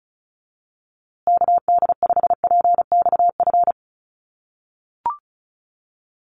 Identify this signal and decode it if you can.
Morse code, heard as a keyed tone, about 34 wpm, 700 Hz. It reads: KD5PXF
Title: USA Call Signs Repeated 3x 35wpm